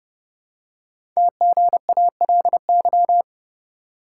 Morse code, heard as a keyed tone, about 28 wpm, 700 Hz. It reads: TGALY
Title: Code Groups - 5 Letters - 30wpm